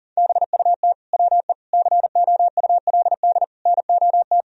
Morse code, heard as a keyed tone, about 40 wpm, 700 Hz. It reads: BUT WE COULD NOT
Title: Sentences from Top 1000 Words - Review Part 2 - 40wpm